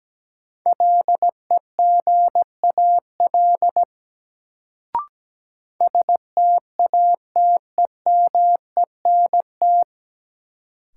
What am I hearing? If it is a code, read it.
Morse code, heard as a keyed tone, about 17 wpm, 700 Hz. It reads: LEGAL STATEMENT
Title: Sentences from Top 1200 Words - New Word Drill Alt - 17wpm